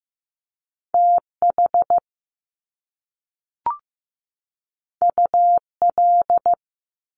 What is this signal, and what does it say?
Morse code, heard as a keyed tone, about 15 wpm, 700 Hz. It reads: TH UL
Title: Top Two Letter Combinations in Words 15wpm